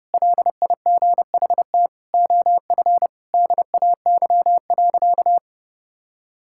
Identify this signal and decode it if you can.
Morse code, heard as a keyed tone, about 29 wpm, 700 Hz. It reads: LIGHT OF DAY.
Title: Sentences from Top 1000 Words - Review Part 5 - 30wpm